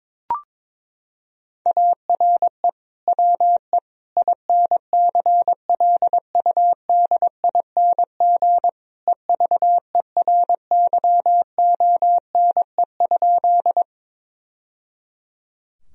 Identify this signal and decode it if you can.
Morse code, heard as a keyed tone, about 22 wpm, 700 Hz. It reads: ARE WE INCLUDING EVERYONE?
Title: Sentences from Top 1000 Words - Review Part 4 - 22wpm